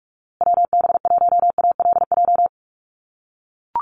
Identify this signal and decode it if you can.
Morse code, heard as a keyed tone, about 44 wpm, 700 Hz. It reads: WB1ALJ
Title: USA Call Signs Repeated 2x 45wpm